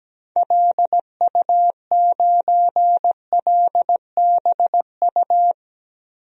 Morse code, heard as a keyed tone, about 17 wpm, 700 Hz. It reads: LU9LBU
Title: DXCC-100 Call Signs - 6 Characters - Repeated 2x - 17wpm